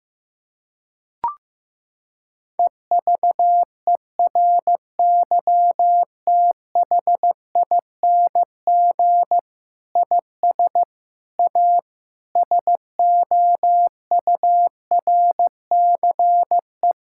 Morse code, heard as a keyed tone, about 15 wpm, 700 Hz. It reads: EVERYTHING IS A SOURCE
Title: Sentences from Top 1000 Words - Review Part 5 - 15wpm